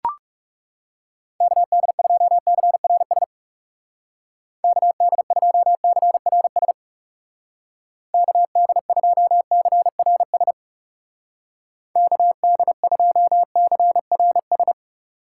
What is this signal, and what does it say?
Morse code, heard as a keyed tone, about 36 wpm, 700 Hz. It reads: KD2CRS KD2CRS KD2CRS KD2CRS
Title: Speed Racing - USA 2x3 Call Signs 30wpm